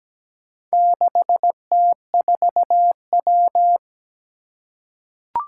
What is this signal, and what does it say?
Morse code, heard as a keyed tone, about 17 wpm, 700 Hz. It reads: 6T4W
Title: Four Challenging Letter-Number Combinations 17wpm